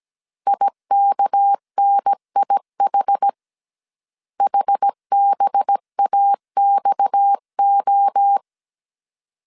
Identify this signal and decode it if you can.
Morse code, heard as a keyed tone, about 17 wpm, 800 Hz. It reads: IKNIH HBAXO
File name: 16wpm